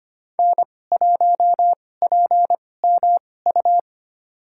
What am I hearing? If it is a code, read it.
Morse code, heard as a keyed tone, about 25 wpm, spaced as for 18 wpm, 700 Hz. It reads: N1PMU
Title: USA Call Signs - 5 Characters - 2x Character Spacing - 25wpm